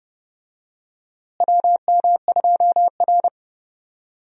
Morse code, heard as a keyed tone, about 30 wpm, 700 Hz. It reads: WM2R
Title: USA Call Signs - 4 Characters - Repeated 3x - 30wpm